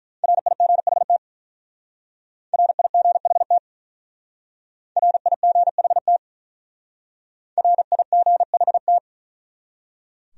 Morse code, heard as a keyed tone, about 43 wpm, 700 Hz. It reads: RIGHT RIGHT RIGHT RIGHT
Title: Speed-Racing Top 200 Words 35wpm